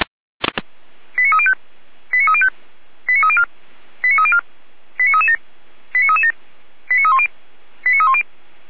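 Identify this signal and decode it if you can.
Five-tone selective call (ZVEI), heard as quick runs of five tones, tones of about 80 ms. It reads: 89305 89305 89304 89304 89337 89337 89310 89310
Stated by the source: Five-tone ZVEI selective call received on an SDR.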